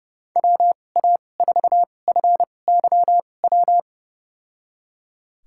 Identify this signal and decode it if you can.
Morse code, heard as a keyed tone, about 32 wpm, spaced as for 22 wpm, 700 Hz. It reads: WA4FYW
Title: USA Call Signs - 6 Characters - 2x Character Spacing - 30wpm